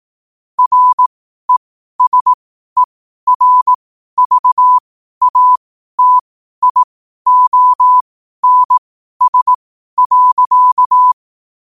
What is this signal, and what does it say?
Morse code, heard as a keyed tone, about 18 wpm, spaced as for 12 wpm, 1000 Hz. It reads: RESERVATIONS.
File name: qotdfiles / 12wpm